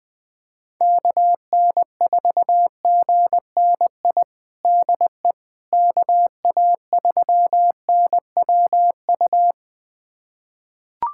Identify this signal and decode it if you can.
Morse code, heard as a keyed tone, about 20 wpm, 700 Hz. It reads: KN4GNI DE KA3NWU
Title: USA Callsign DE Callsign - 6 Characters - 20wpm